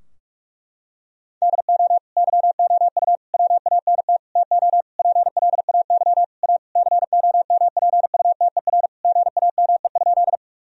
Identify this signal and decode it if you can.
Morse code, heard as a keyed tone, about 46 wpm, 700 Hz. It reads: DO YOU WANT TO PLAY A COMPUTER GAME?
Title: Sentences from Top 1000 Words - Review Part 3 - 45wpm